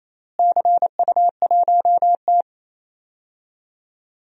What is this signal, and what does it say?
Morse code, heard as a keyed tone, about 27 wpm, 700 Hz. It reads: CU1T
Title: DXCC-100 Call Signs - 4 Characters - Repeated 3x - 28wpm